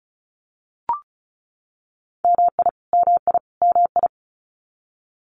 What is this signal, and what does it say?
Morse code, heard as a keyed tone, about 34 wpm, 700 Hz. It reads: MI MI MI
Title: US States Repeated 3x 35wpm